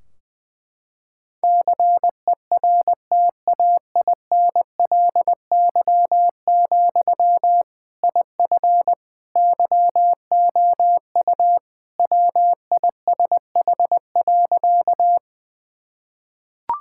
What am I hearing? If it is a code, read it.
Morse code, heard as a keyed tone, about 21 wpm, 700 Hz. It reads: CERTAINLY, IF YOU WISH.
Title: Sentences from Top 1600 Words 20wpm